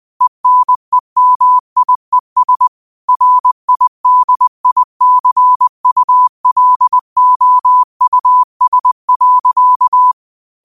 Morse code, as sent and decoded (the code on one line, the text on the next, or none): . -. -- .. ... / .-. .. -.. .. -.-. ..- .-.. --- ..- ... .-.-.-
ENEMIES RIDICULOUS.